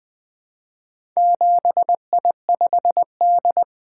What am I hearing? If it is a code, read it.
Morse code, heard as a keyed tone, about 20 wpm, 700 Hz. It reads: 7I5D
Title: Four Letter-Number Combinations - 20wpm